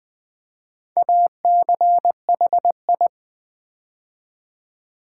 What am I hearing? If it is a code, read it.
Morse code, heard as a keyed tone, about 20 wpm, 700 Hz. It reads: ACHI
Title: Build A Word - 4 Letters At A Time - 11-Letter Words - 20wpm